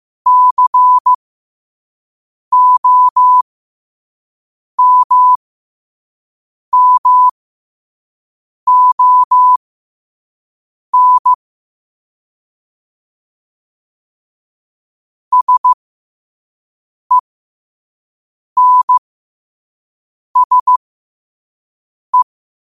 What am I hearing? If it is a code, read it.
Morse code, heard as a keyed tone, about 15 wpm, spaced as for 5 wpm, 1000 Hz. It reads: COMMON SENSE
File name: qotdfiles / 5wpm